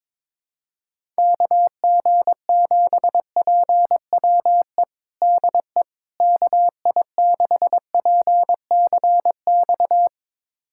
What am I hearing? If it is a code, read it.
Morse code, heard as a keyed tone, about 22 wpm, 700 Hz. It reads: KG7PWE DE KI6PCX
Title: USA Callsign DE Callsign - 6 Characters - 22wpm